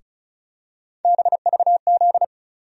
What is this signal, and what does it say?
Morse code, heard as a keyed tone, about 34 wpm, 700 Hz. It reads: BVZ
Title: USA Call Sign Postfixes - 3 Letters 35wpm